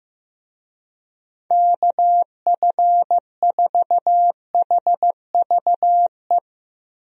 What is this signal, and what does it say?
Morse code, heard as a keyed tone, about 15 wpm, 700 Hz. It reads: KF4HVE
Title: USA Call Signs - 6 Characters - Repeated 2x - 15wpm